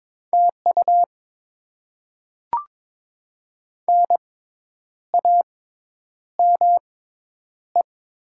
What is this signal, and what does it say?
Morse code, heard as a keyed tone, about 22 wpm, spaced as for 9 wpm, 690 Hz. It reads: TU NAME
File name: Top 20 QSO Elements - 6x Character Spacing - 22wpm